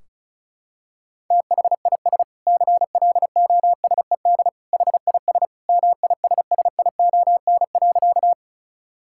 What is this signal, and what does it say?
Morse code, heard as a keyed tone, about 36 wpm, 700 Hz. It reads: THIS CLOSED HIS MISSION.
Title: Sentences from Top 1600 Words 35wpm